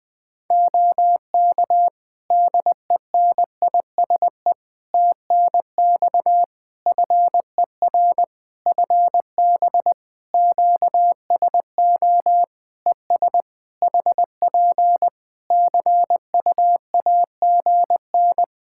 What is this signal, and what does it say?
Morse code, heard as a keyed tone, about 20 wpm, 700 Hz. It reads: OK DENISE TNX FER FB QSO ES HP CUAGN
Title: K1USN SST QSO Practice 20wpm